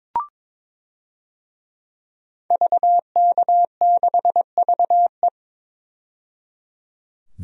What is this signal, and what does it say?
Morse code, heard as a keyed tone, about 22 wpm, 700 Hz. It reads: VK6VE
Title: DXCC-100 Call Signs - 5 Characters - 22wpm